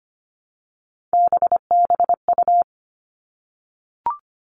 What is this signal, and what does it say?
Morse code, heard as a keyed tone, about 25 wpm, 700 Hz. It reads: BBU
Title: All Cut Numbers - 3-Digit 25wpm